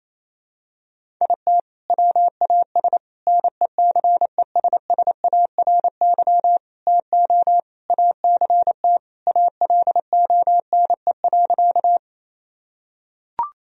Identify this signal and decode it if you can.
Morse code, heard as a keyed tone, about 28 wpm, 700 Hz. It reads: IT WAS NECESSARY TO ACT ALONE.
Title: Sentences from Top 1000 Words - Review Part 1 - 28wpm